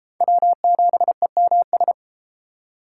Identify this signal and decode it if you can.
Morse code, heard as a keyed tone, about 32 wpm, 700 Hz. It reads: W7EMS